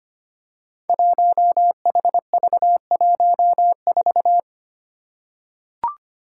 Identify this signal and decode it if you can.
Morse code, heard as a keyed tone, about 25 wpm, 700 Hz. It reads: 1HV14
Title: Five Challenging Letter-Number Combinations 25wpm